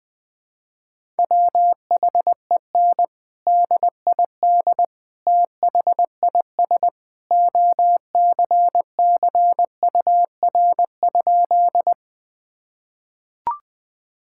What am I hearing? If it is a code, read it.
Morse code, heard as a keyed tone, about 21 wpm, 700 Hz. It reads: WHEN DID THIS OCCUR?
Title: Sentences from Top 1000 Words - Review Part 6 - 20wpm